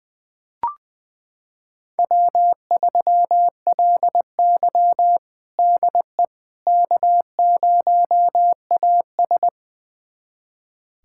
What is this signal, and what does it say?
Morse code, heard as a keyed tone, about 20 wpm, 700 Hz. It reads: W3LY DE K0AS
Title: USA Callsign DE Callsign - 4 Characters - 20wpm